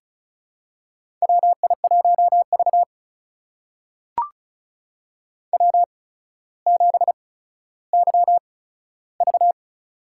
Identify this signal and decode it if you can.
Morse code, heard as a keyed tone, about 36 wpm, spaced as for 15 wpm, 700 Hz. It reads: WI1V W7YV
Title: USA Call Signs - 4 Characters - 8x Character Spacing - 35wpm